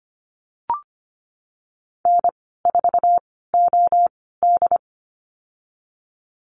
Morse code, heard as a keyed tone, about 25 wpm, spaced as for 16 wpm, 700 Hz. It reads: N4OD
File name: Lesson 15g - Number 5 - Call Signs 25wpm